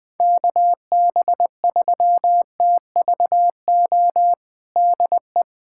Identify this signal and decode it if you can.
Morse code, heard as a keyed tone, about 21 wpm, 700 Hz. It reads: KB3TVO DE
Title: USA Callsign DE Callsign - 6 Characters - 20wpm